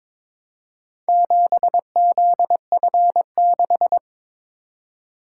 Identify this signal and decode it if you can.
Morse code, heard as a keyed tone, about 22 wpm, 700 Hz. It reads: 7ZF6
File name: practice / 22wpm / Four Letter-Number Combinations - 22wpm